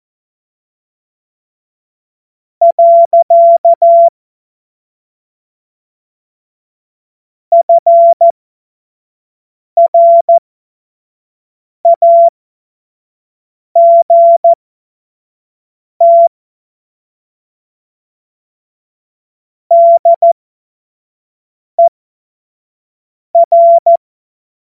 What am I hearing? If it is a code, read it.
Morse code, heard as a keyed tone, about 14 wpm, spaced as for 5 wpm, 680 Hz. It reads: . FRAGT DER
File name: Witz-05-August-2025-25-F70